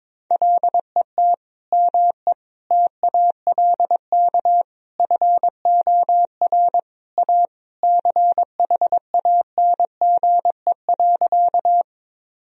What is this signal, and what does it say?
Morse code, heard as a keyed tone, about 21 wpm, 700 Hz. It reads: LET ME TALK FOR A CHANGE.
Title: Sentences from Top 1000 Words - Review Part 1 - 22wpm